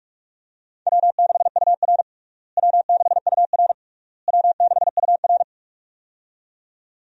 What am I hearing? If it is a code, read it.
Morse code, heard as a keyed tone, about 47 wpm, 700 Hz. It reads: W6UR W6UR W6UR
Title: USA Call Signs - 4 Characters - Repeated 3x - 45wpm